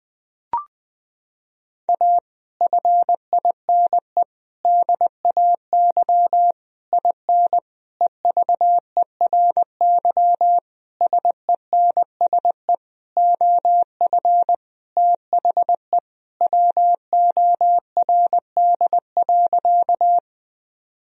Morse code, heard as a keyed tone, about 20 wpm, 700 Hz. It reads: A FINE DAY IN EVERY SENSE OF THE WORD.
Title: Sentences from Top 1000 Words - Review Part 4 - 20wpm